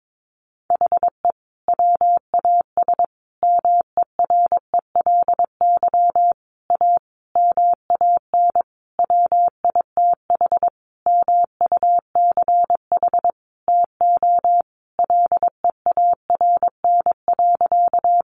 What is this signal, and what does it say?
Morse code, heard as a keyed tone, about 22 wpm, 700 Hz. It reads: HE WAS MERELY A MAN WITH MUCH TO LEARN.
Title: Sentences from Top 1400 Words 22wpm